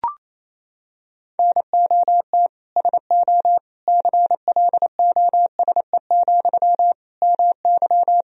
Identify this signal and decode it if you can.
Morse code, heard as a keyed tone, about 28 wpm, 700 Hz. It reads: NOT SO CLOSE, MY
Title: Sentences from Top 1000 Words - Review Part 1 - 28wpm